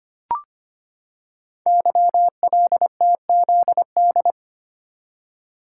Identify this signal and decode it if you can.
Morse code, heard as a keyed tone, about 25 wpm, 700 Hz. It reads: YLTZD
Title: Code Groups - 5 Letter:Numbers - 25wpm